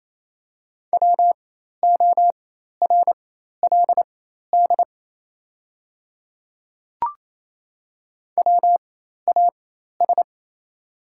Morse code, heard as a keyed tone, about 28 wpm, spaced as for 13 wpm, 700 Hz. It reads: WORLD WAS
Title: 4x Character Spacing - Top 300 Words - 28wpm